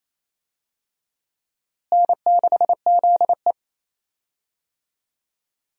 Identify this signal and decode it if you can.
Morse code, heard as a keyed tone, about 28 wpm, 700 Hz. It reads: N6ZE
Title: USA Call Signs - 4 Characters - Repeated 3x - 28wpm